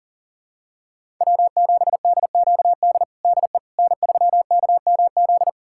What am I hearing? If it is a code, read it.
Morse code, heard as a keyed tone, about 37 wpm, 700 Hz. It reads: W7DQD DE N3KMZ
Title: USA Callsign DE Callsign - 5 Characters - 40wpm